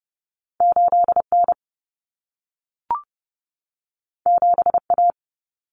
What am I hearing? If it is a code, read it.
Morse code, heard as a keyed tone, about 30 wpm, 700 Hz. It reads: 8N 7A
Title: Common Cut Numbers - 2-Digit 30wpm